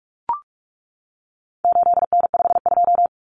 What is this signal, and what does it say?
Morse code, heard as a keyed tone, about 44 wpm, 700 Hz. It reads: 8N52
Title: Common Cut Numbers - 4-Digit 45wpm